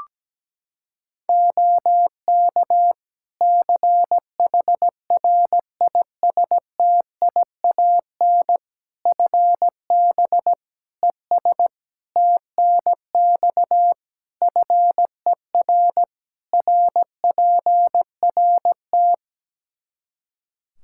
Morse code, heard as a keyed tone, about 17 wpm, 700 Hz. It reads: OK CHRISTIAN FB ES TNX FER RPRT
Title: Practice QSO - Exchange 2 - Info - 17wpm